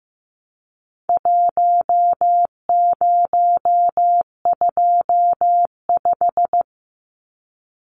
Morse code, heard as a keyed tone, about 15 wpm, 700 Hz. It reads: 1025
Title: Numbers - Contest Style - 4-Digit 15wpm